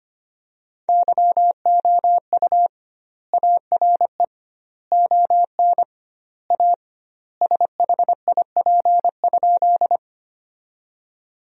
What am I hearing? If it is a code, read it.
Morse code, heard as a keyed tone, about 25 wpm, 700 Hz. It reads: YOU ARE ON A SHIP?
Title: 2x Word-Spacing - Sentences from Top 400 Words - 25wpm